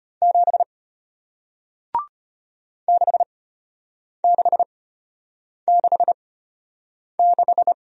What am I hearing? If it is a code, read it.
Morse code, heard as a keyed tone, about 33 wpm, spaced as for 8 wpm, 700 Hz. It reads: 7 6666